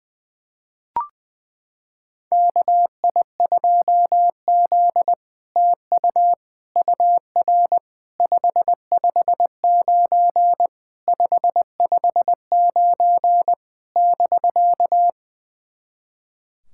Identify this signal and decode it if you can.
Morse code, heard as a keyed tone, about 21 wpm, 700 Hz. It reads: KI2Z TU UR 559 559 <BK>
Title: SOTA QSO Practice 20wpm